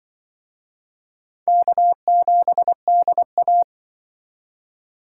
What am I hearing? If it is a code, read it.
Morse code, heard as a keyed tone, about 25 wpm, 700 Hz. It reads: K7DA
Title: Speed Racing - USA 1x2 Call Signs 20wpm